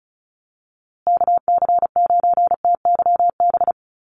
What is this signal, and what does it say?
Morse code, heard as a keyed tone, about 34 wpm, 700 Hz. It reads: KC9TYB
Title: USA Call Signs Repeated 3x 35wpm